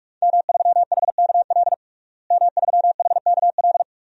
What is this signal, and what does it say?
Morse code, heard as a keyed tone, about 46 wpm, 700 Hz. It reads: M3HKL M3HKL
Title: DXCC-100 Call Signs - 5 Characters - Repeated 2x - 45wpm